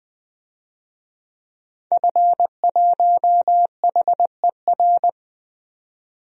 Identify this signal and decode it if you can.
Morse code, heard as a keyed tone, about 20 wpm, 700 Hz. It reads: F1HER
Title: DXCC-100 Call Signs - 5 Characters - 20wpm